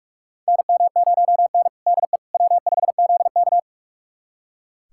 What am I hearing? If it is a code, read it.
Morse code, heard as a keyed tone, about 46 wpm, 700 Hz. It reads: NM0N DE W5ZK
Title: USA Callsign DE Callsign - 4 Characters - 45wpm